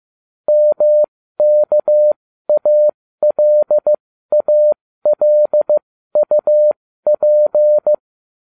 Morse code, heard as a keyed tone, about 15 wpm, spaced as for 13 wpm, 600 Hz. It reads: MKALALUP